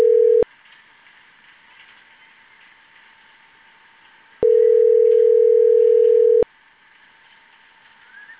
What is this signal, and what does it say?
Ringback tone (North America), 440 + 480 Hz.